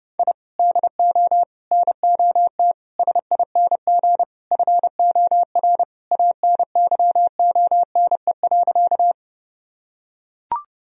Morse code, heard as a keyed tone, about 30 wpm, 700 Hz. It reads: I DO NOT SING FOR ANYONE.
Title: Sentences from Top 1000 Words - Review Part 2 - 30wpm